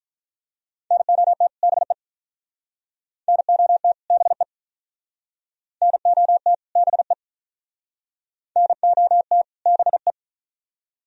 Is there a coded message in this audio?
Yes: Morse code from 0.9 to 10.1 s, heard as a keyed tone, about 46 wpm, 700 Hz. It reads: NOT BE NOT BE NOT BE NOT BE